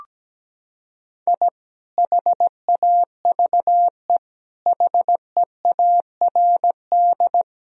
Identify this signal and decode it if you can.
Morse code, heard as a keyed tone, about 17 wpm, 700 Hz. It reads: I HAVE HEARD
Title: Sentences from Top 1000 Words - Review Part 3 - 17wpm